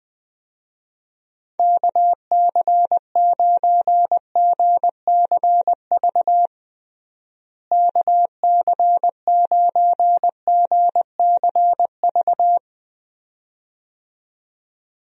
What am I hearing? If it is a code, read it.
Morse code, heard as a keyed tone, about 21 wpm, 700 Hz. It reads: KC9GCV KC9GCV
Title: USA Call Signs - 6 Characters - Repeated 2x - 20wpm